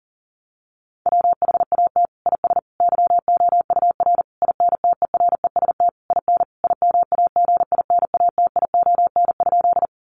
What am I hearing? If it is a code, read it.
Morse code, heard as a keyed tone, about 40 wpm, 700 Hz. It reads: WHAT IS YOUR INTEREST IN IMAGINATION?
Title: Sentences from Top 1300 Words 40wpm